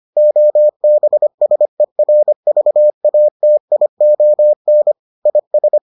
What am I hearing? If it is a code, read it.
Morse code, heard as a keyed tone, about 25 wpm, 600 Hz. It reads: OBSERVATION IS